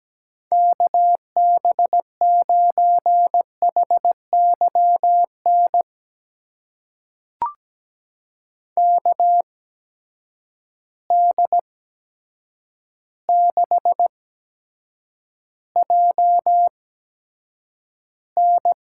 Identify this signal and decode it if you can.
Morse code, heard as a keyed tone, about 17 wpm, spaced as for 8 wpm, 700 Hz. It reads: KB9HYN KD6JN
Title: USA Call Signs - 6 Characters - 8x Character Spacing - 17wpm